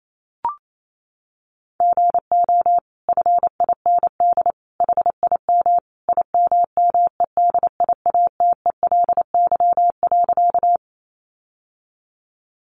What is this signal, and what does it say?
Morse code, heard as a keyed tone, about 28 wpm, 700 Hz. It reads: GO FIND HIM IMMEDIATELY.
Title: Sentences from Top 1100 Words 28wpm